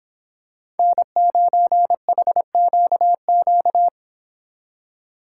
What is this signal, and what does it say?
Morse code, heard as a keyed tone, about 26 wpm, 700 Hz. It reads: N9HQQ